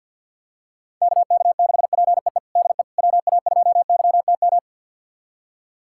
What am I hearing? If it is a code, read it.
Morse code, heard as a keyed tone, about 52 wpm, 700 Hz. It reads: KK6PEE DE WA2YTM
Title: USA Callsign DE Callsign - 6 Characters - 50wpm